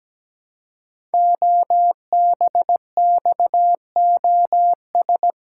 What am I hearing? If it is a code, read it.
Morse code, heard as a keyed tone, about 17 wpm, 700 Hz. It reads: OBXOS
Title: Code Groups - 5 Letters - 17wpm